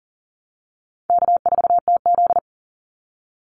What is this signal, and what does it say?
Morse code, heard as a keyed tone, about 37 wpm, 700 Hz. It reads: K4TZ
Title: Easy Call Signs Repeated 2x - 40wpm